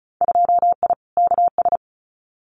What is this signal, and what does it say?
Morse code, heard as a keyed tone, about 36 wpm, 700 Hz. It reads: 2I KS
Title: Winter Field Day Exchanges Encore 35wpm